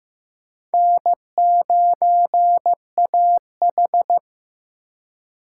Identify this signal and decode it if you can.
Morse code, heard as a keyed tone, about 15 wpm, 700 Hz. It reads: N9AH
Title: USA Call Signs - 3x Character Spacing - 15wpm